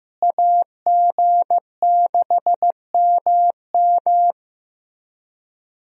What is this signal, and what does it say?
Morse code, heard as a keyed tone, about 14 wpm, 700 Hz. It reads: AG6MM
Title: USA Call Signs - 5 Characters - Repeated 3x - 15wpm